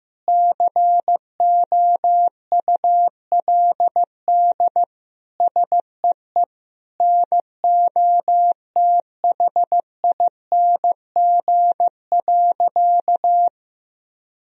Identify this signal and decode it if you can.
Morse code, heard as a keyed tone, about 15 wpm, 700 Hz. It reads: COULD SEE NOTHING.
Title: Sentences from Top 1000 Words - Review Part 2 - 15wpm